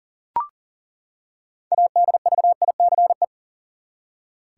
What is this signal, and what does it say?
Morse code, heard as a keyed tone, about 40 wpm, 700 Hz. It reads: ADVICE
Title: Six Letter Words - 40wpm